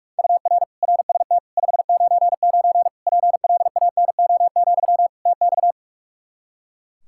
Morse code, heard as a keyed tone, about 46 wpm, 700 Hz. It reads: UR RST 599 PLANO, TX
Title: K1USN SST QSO Practice 45wpm